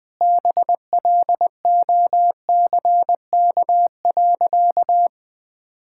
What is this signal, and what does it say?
Morse code, heard as a keyed tone, about 21 wpm, 700 Hz. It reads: BLOCK.
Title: Easy Sentences with Hints - 20wpm